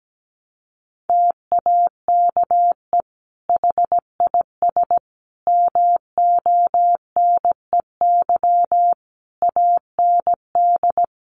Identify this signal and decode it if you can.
Morse code, heard as a keyed tone, about 17 wpm, 700 Hz. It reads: TAKE HIS MONEY AND
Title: Sentences from Top 1300 Words 17wpm